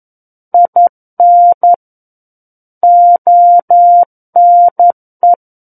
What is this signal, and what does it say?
Morse code, heard as a keyed tone, about 11 wpm, 700 Hz. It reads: IN ONE
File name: Book One / 02-Chapter 2 - 11 WPM